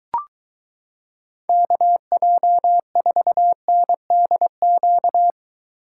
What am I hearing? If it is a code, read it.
Morse code, heard as a keyed tone, about 22 wpm, 700 Hz. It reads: KJ4NDQ